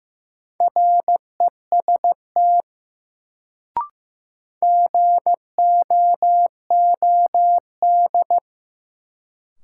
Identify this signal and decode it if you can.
Morse code, heard as a keyed tone, about 15 wpm, 700 Hz. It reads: REST GOOD
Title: Sets of 2 Words - New Word Drill - 15wpm